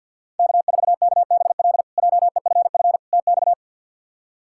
Morse code, heard as a keyed tone, about 48 wpm, 700 Hz. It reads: K4KBL JEFF TX
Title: K1USN SST QSO Practice - 50wpm